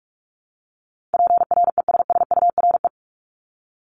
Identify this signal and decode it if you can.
Morse code, heard as a keyed tone, about 46 wpm, 700 Hz. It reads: PRESSURE
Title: Multi-letter Phonemes 45wpm